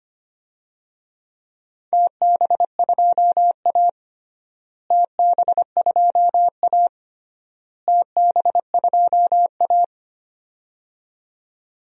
Morse code, heard as a keyed tone, about 25 wpm, 700 Hz. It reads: TB2A TB2A TB2A
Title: DXCC-100 Call Signs - 4 Characters - Repeated 3x - 25wpm